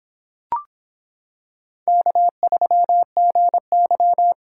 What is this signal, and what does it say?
Morse code, heard as a keyed tone, about 26 wpm, 700 Hz. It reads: K3GY